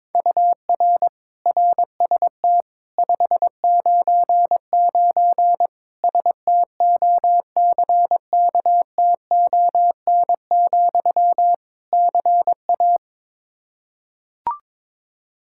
Morse code, heard as a keyed tone, about 22 wpm, 700 Hz. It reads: UR RST 599 STOCKTON, CA
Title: K1USN SST QSO Practice 22wpm